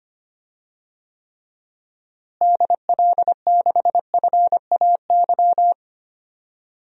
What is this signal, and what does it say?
Morse code, heard as a keyed tone, about 25 wpm, 700 Hz. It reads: DL6FAY
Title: DXCC-100 Call Signs - 6 Characters - Repeated 3x - 25wpm